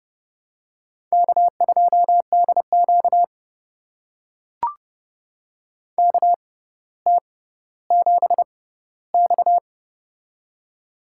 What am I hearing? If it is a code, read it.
Morse code, heard as a keyed tone, about 30 wpm, spaced as for 15 wpm, 700 Hz. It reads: K2DQ KT7X
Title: USA Call Signs - 4 Characters - 6x Character Spacing - 30wpm